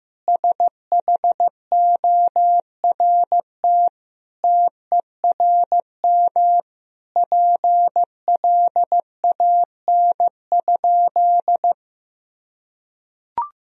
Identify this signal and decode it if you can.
Morse code, heard as a keyed tone, about 15 wpm, 700 Hz. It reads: SHORT TERM PLAN?
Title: Sentences from Top 1000 Words - Review Part 4 - 15wpm